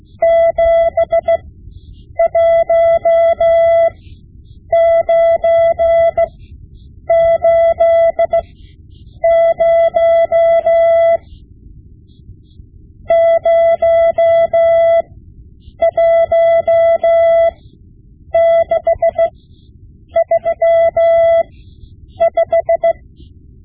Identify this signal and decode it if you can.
Morse code, heard as a keyed tone, about 16 wpm, spaced as for 8 wpm, 670 Hz. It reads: 71980 01635
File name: Morse Lessons / Morse Lessons / Tape-5B